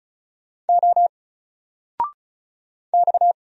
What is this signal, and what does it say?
Morse code, heard as a keyed tone, about 36 wpm, 700 Hz. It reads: O X
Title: Single Character (Letter:Number:Punctuation:Prosign) 35wpm